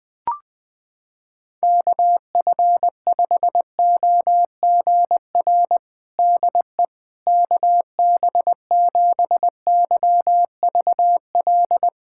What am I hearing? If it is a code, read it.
Morse code, heard as a keyed tone, about 20 wpm, 700 Hz. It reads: KF5OGR DE KB7YVL
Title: USA Callsign DE Callsign - 6 Characters - 20wpm